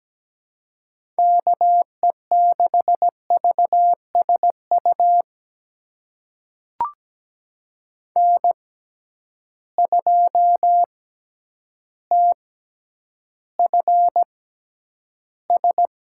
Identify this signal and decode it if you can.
Morse code, heard as a keyed tone, about 17 wpm, spaced as for 9 wpm, 700 Hz. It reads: KE6VSU N2TFS
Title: USA Call Signs - 6x Character Spacing - 17wpm